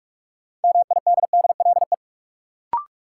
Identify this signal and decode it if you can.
Morse code, heard as a keyed tone, about 46 wpm, 700 Hz. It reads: MIDDLE
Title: Sentences from Top 900 Words - New Word Drill - 45wpm